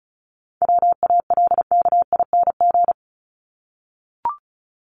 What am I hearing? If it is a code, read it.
Morse code, heard as a keyed tone, about 34 wpm, 700 Hz. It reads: WALKING
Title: ING Suffix - Top 100 Words with Base Words 35wpm